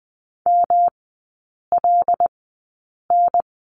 Morse code, heard as a keyed tone, about 20 wpm, spaced as for 8 wpm, 700 Hz. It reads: MLN
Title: 114 Lesson 19e - Letter w - Three Characters 20wpm